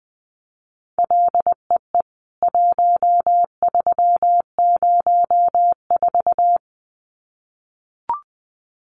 Morse code, heard as a keyed tone, about 20 wpm, 700 Hz. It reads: LEE 1304
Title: CWT Exchange 20wpm